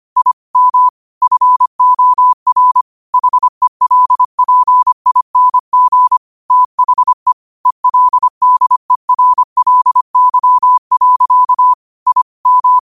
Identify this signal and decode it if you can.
Morse code, heard as a keyed tone, about 25 wpm, 1000 Hz. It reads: I M FOR HELPING THE ELDERLY. I M
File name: qotdfiles / 25wpm